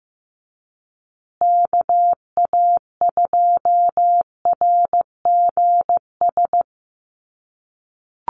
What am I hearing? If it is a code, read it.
Morse code, heard as a keyed tone, about 15 wpm, 700 Hz. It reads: KA2RGS
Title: USA Call Signs Repeated 3x 15wpm